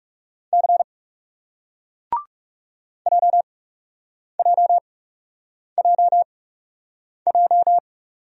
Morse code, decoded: C JJJJ

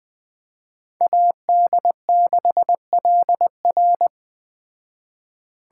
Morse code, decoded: AD6LR

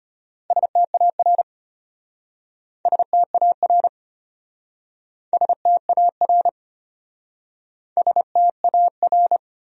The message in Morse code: STAR STAR STAR STAR

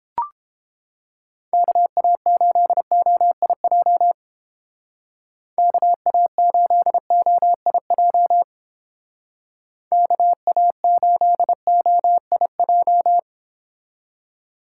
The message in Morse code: KA8OIJ KA8OIJ KA8OIJ

